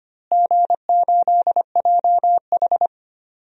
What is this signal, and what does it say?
Morse code, heard as a keyed tone, about 25 wpm, 700 Hz. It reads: G8JH
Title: DXCC-100 Call Signs - 4 Characters - Repeated 3x - 25wpm